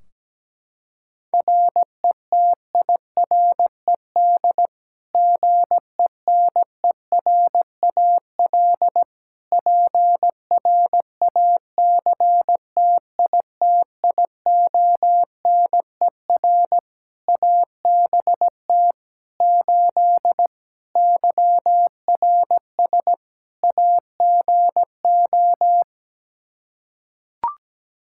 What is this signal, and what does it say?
Morse code, heard as a keyed tone, about 17 wpm, 700 Hz. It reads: RETIRED GENERAL PRACTITIONER ABT 8 YRS AGO
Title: Practice QSO - Exchange 3 - More Info - 17wpm